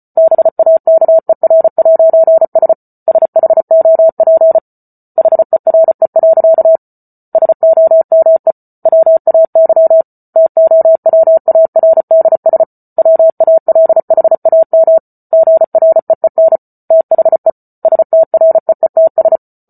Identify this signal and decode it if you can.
Morse code, heard as a keyed tone, about 34 wpm, 650 Hz. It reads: BAKER'S SHOP HERE. SOME WAY TOWARDS WALHAM GREEN THE STREETS